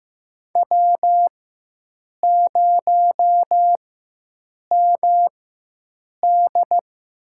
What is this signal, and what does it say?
Morse code, heard as a keyed tone, about 15 wpm, spaced as for 7 wpm, 700 Hz. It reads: W0MD
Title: USA Call Signs - 4x Character Spacing - 15wpm